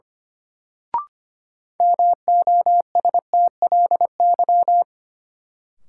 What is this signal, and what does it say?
Morse code, heard as a keyed tone, about 25 wpm, 700 Hz. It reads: MOSTLY
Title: Sentences from Top 1300 Words - New Word Drill Alt - 25wpm